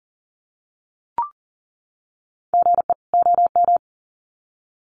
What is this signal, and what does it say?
Morse code, heard as a keyed tone, about 40 wpm, 700 Hz. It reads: GE OM
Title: Common QSO Phrases 40wpm